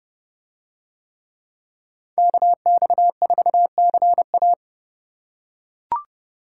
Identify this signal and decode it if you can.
Morse code, heard as a keyed tone, about 29 wpm, 700 Hz. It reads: KX4CA